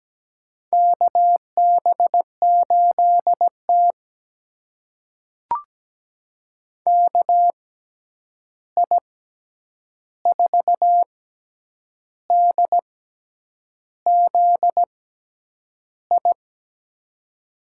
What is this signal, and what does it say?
Morse code, heard as a keyed tone, about 17 wpm, spaced as for 8 wpm, 700 Hz. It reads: KB8T KI4DZI
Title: USA Call Signs - 6x Character Spacing - 17wpm